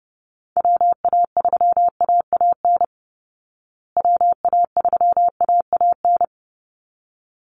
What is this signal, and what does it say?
Morse code, heard as a keyed tone, about 30 wpm, 700 Hz. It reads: WA3AAN WA3AAN
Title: Call Signs Repeated 3x - 30wpm